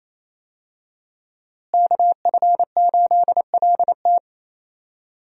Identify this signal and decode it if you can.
Morse code, heard as a keyed tone, about 28 wpm, 700 Hz. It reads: KF8LT
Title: USA Call Signs - 5 Characters - Repeated 3x - 28wpm